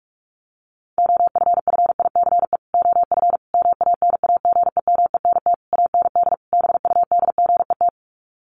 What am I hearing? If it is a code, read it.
Morse code, heard as a keyed tone, about 44 wpm, 700 Hz. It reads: OFFICE OF MANAGEMENT AND BUDGET
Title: Sets of 5 Words Encore2 45wpm